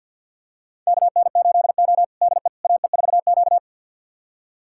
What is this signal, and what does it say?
Morse code, heard as a keyed tone, about 48 wpm, 700 Hz. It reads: KN8O DE AE4Q
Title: USA Callsign DE Callsign - 4 Characters - 50wpm